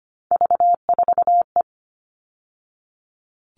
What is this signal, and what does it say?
Morse code, heard as a keyed tone, about 25 wpm, 700 Hz. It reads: V4E
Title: All Cut Numbers - 3-Digit 25wpm